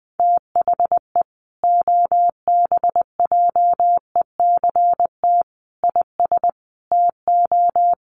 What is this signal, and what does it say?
Morse code, heard as a keyed tone, about 20 wpm, 700 Hz. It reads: THE OBJECT IS TO
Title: Sentences from Top 1300 Words 20wpm